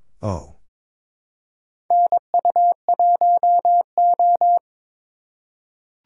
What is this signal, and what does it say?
Morse code, heard as a keyed tone, about 22 wpm, 700 Hz. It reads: NU1O
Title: K1USN SST QSO Practice - 22wpm